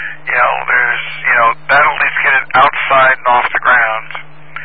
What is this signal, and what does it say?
Emergency Alert System (SAME) tones, heard as harsh buzzing bursts, with speech. It is heard from 0.6 to 3.9 s.